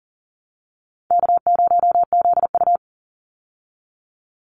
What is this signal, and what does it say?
Morse code, heard as a keyed tone, about 40 wpm, 700 Hz. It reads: K0ZU
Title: Easy Call Signs Repeated 2x - 40wpm